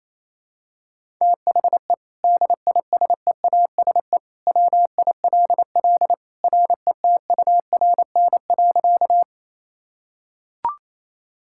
Morse code, heard as a keyed tone, about 28 wpm, 700 Hz. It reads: THE DISEASE WILL RETURN.
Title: Sentences from Top 1000 Words - Review Part 6 - 28wpm